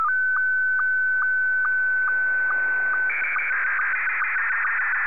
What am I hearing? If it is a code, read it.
SSTV (slow-scan television), Scottie 1.
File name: sstv / S1-VK3DEI